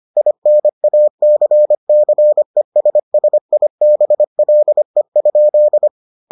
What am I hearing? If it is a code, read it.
Morse code, heard as a keyed tone, about 25 wpm, 600 Hz. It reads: INACCESSIBLE?